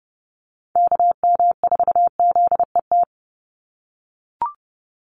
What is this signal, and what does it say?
Morse code, heard as a keyed tone, about 30 wpm, 700 Hz. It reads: KM4ZET
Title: USA Call Signs 30wpm